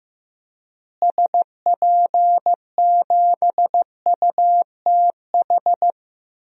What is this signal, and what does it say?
Morse code, heard as a keyed tone, about 15 wpm, 700 Hz. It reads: SP7UTH
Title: DXCC-100 Call Signs - 6 Characters - Repeated 2x - 15wpm